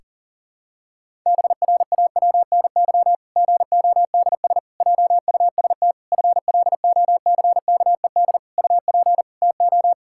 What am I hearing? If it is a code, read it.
Morse code, heard as a keyed tone, about 40 wpm, 700 Hz. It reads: BRAWNY GODS JUST FLOCKED UP TO
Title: Pangram Challenge v2 - 40wpm